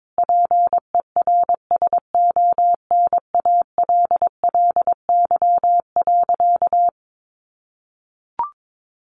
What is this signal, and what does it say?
Morse code, heard as a keyed tone, about 22 wpm, 700 Hz. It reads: PERSONALLY.
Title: Sentences from Top 1400 Words 22wpm